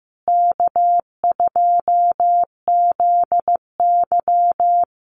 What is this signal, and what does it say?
Morse code, heard as a keyed tone, about 15 wpm, 700 Hz. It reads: K2ZY
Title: USA Call Signs Repeated 2x 15wpm